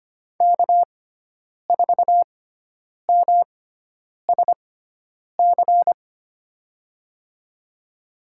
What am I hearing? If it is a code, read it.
Morse code, heard as a keyed tone, about 25 wpm, spaced as for 9 wpm, 700 Hz. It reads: K4MSC
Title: USA Call Signs - 5 Characters - 6x Character Spacing - 25wpm